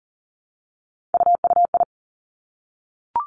Morse code, heard as a keyed tone, about 38 wpm, 700 Hz. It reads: UUI
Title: Three Challenging Letter-Number Combinations 40wpm